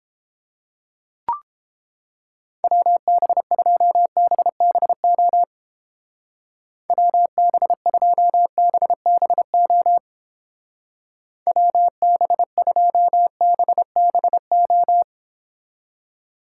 Morse code, heard as a keyed tone, about 31 wpm, 700 Hz. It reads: WB2BBO WB2BBO WB2BBO